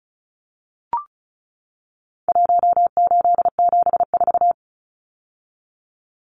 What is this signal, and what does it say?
Morse code, heard as a keyed tone, about 34 wpm, 700 Hz. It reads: 1874
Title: Numbers - Contest Style - 4-Digit 35wpm